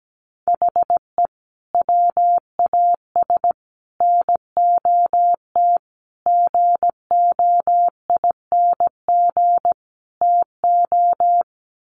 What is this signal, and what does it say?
Morse code, heard as a keyed tone, about 17 wpm, 700 Hz. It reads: HE WAS NOT GOING TO
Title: Sets of 5 Words 17wpm